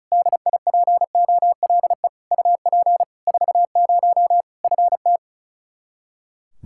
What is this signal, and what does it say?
Morse code, heard as a keyed tone, about 36 wpm, 700 Hz. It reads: DIPOLE UP 40 FT
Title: Practice QSO - Exchange 2 - Info - 35wpm